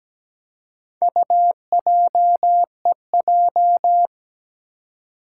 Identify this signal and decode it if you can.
Morse code, heard as a keyed tone, about 17 wpm, 700 Hz. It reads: UJEJ
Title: Four Challenging Letter-Number Combinations 17wpm